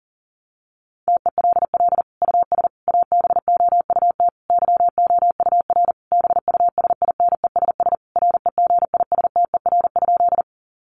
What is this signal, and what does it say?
Morse code, heard as a keyed tone, about 40 wpm, 700 Hz. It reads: TELL US ABOUT YOUR BUSINESS REGISTER?
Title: Sentences from Top 1200 Words 40wpm